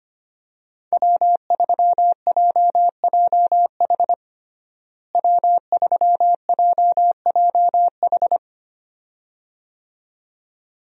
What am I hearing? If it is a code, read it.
Morse code, heard as a keyed tone, about 25 wpm, 700 Hz. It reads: W3JJH W3JJH
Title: USA Call Signs - 5 Characters - Repeated 2x - 25wpm